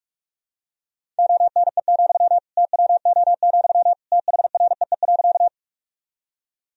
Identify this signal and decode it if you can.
Morse code, heard as a keyed tone, about 42 wpm, 700 Hz. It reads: ONE, TWO, THREE.
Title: Sentences from Top 1000 Words - Review Part 1 - 45wpm